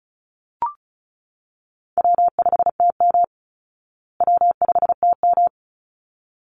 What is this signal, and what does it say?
Morse code, heard as a keyed tone, about 34 wpm, 700 Hz. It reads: W5TM W5TM
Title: Easy Call Signs Repeated 2x - 35wpm